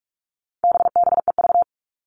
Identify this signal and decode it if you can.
Morse code, heard as a keyed tone, about 44 wpm, 700 Hz. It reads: BBEV
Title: All Cut Numbers - 4-Digit 45wpm